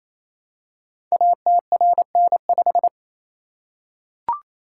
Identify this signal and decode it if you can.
Morse code, heard as a keyed tone, about 29 wpm, 700 Hz. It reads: ATRN5
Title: Code Groups - 5 Letter:Numbers - 28wpm